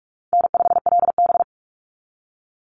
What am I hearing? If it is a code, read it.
Morse code, heard as a keyed tone, about 44 wpm, 700 Hz. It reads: N5LB
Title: Call Signs - 45wpm